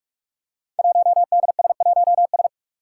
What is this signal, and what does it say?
Morse code, heard as a keyed tone, about 43 wpm, 700 Hz. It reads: 1DS1S
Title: Five Challenging Letter-Number Combinations 45wpm